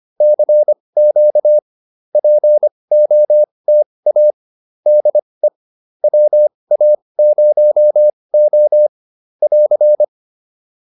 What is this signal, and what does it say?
Morse code, heard as a keyed tone, about 25 wpm, spaced as for 20 wpm, 600 Hz. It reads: CQ POTA DE WA0O <AR>